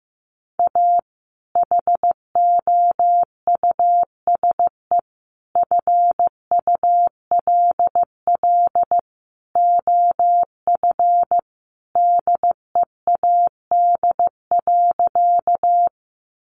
Morse code, heard as a keyed tone, about 15 wpm, 700 Hz. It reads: A HOUSE FULL OF DEAD.
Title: Sentences from Top 1500 Words 15wpm